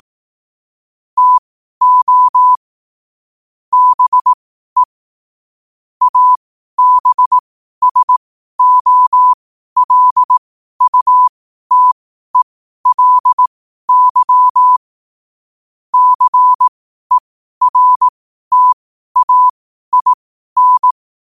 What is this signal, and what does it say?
Morse code, heard as a keyed tone, about 18 wpm, spaced as for 12 wpm, 1000 Hz. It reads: TO BE ABSOLUTELY CERTAIN
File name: qotdfiles / 12wpm